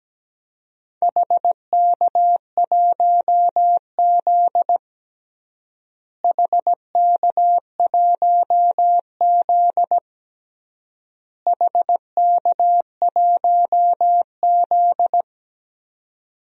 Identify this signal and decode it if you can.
Morse code, heard as a keyed tone, about 17 wpm, 700 Hz. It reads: HK1Z HK1Z HK1Z
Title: DXCC-100 Call Signs - 4 Characters - Repeated 3x - 17wpm